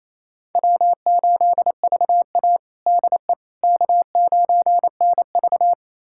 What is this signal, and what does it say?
Morse code, heard as a keyed tone, about 28 wpm, 700 Hz. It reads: W8VA DE K9NV
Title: USA Callsign DE Callsign - 4 Characters - 28wpm